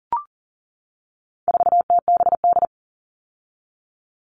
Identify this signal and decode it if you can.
Morse code, heard as a keyed tone, about 39 wpm, 700 Hz. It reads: 4TBD
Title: All Cut Numbers - 4-Digit 40wpm